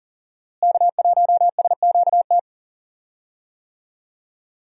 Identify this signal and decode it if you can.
Morse code, heard as a keyed tone, about 40 wpm, 700 Hz. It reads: K1SQT
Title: USA Call Signs - 5 Characters - Repeated 3x - 40wpm